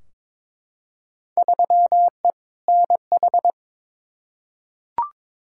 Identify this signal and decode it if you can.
Morse code, heard as a keyed tone, about 21 wpm, 700 Hz. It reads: 3E NH
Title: Field Day Exchanges Encore - 22wpm